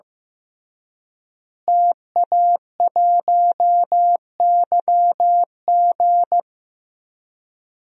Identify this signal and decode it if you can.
Morse code, heard as a keyed tone, about 15 wpm, 700 Hz. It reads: TA1YG
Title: DXCC-100 Call Signs - 5 Characters - Repeated 3x - 15wpm